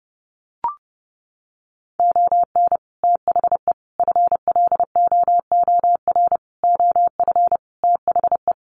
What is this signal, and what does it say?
Morse code, heard as a keyed tone, about 30 wpm, 700 Hz. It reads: ON THE FLOOR OF THE
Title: Sets of 5 Words 30wpm